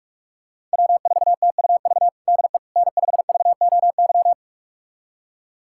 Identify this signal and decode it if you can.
Morse code, heard as a keyed tone, about 46 wpm, 700 Hz. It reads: W4TUV DE N5VOY
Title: USA Callsign DE Callsign - 5 Characters - 45wpm